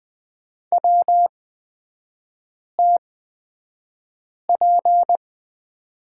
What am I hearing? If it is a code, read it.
Morse code, heard as a keyed tone, about 20 wpm, spaced as for 5 wpm, 700 Hz. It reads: WTP
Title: 224 Lesson 36e - BT - Three Characters 20wpm